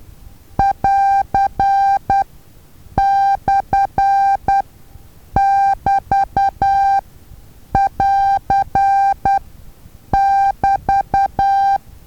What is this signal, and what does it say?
Morse code, heard as a keyed tone, about 10 wpm, spaced as for 7 wpm, 790 Hz. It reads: <AR>/<BT><AR><BT>
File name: Taviro HA6VA A